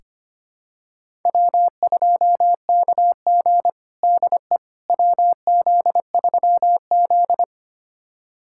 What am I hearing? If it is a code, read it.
Morse code, heard as a keyed tone, about 25 wpm, 700 Hz. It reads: W2KG DE WZ3Z
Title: USA Callsign DE Callsign - 4 Characters - 25wpm